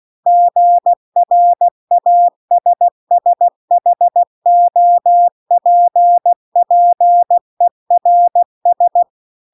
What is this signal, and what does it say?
Morse code, heard as a keyed tone, about 16 wpm, 700 Hz. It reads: GRASSHOPPERS
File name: CW-The-Big-Twoheaded-River-2-16WPM